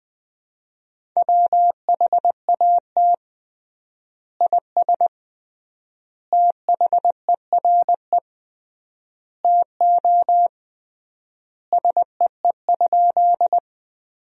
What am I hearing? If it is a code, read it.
Morse code, heard as a keyed tone, about 21 wpm, 700 Hz. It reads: WHAT IS THERE TO SEE?
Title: 3x Word-Spacing - Sentences from Top 100 Words - 20wpm